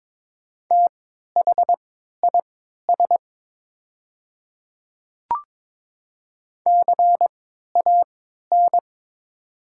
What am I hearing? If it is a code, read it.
Morse code, heard as a keyed tone, about 22 wpm, spaced as for 12 wpm, 700 Hz. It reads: THIS CAN
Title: 3x Character Spacing - Top 200 Words - 22wpm